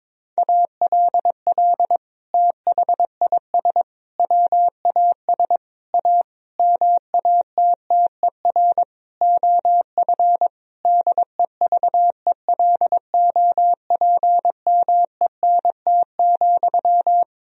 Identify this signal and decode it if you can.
Morse code, heard as a keyed tone, about 22 wpm, 700 Hz. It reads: ALL THIS WAS A MATTER OF DEVELOPMENT,
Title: Sentences from Top 1000 Words - Review Part 4 - 22wpm